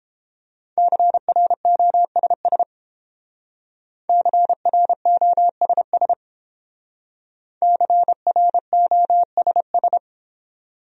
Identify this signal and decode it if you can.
Morse code, heard as a keyed tone, about 29 wpm, 700 Hz. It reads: CROSS CROSS CROSS